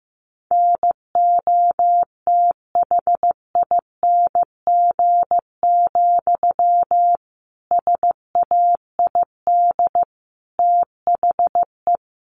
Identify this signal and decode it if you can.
Morse code, heard as a keyed tone, about 15 wpm, 700 Hz. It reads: NOTHING, SAID THE
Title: Sentences from Top 1300 Words 15wpm